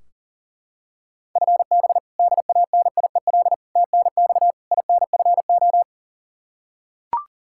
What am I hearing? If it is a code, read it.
Morse code, heard as a keyed tone, about 40 wpm, 700 Hz. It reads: FB DANIEL TNX INFO
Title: Practice QSO - Exchange 3 - More Info - 40wpm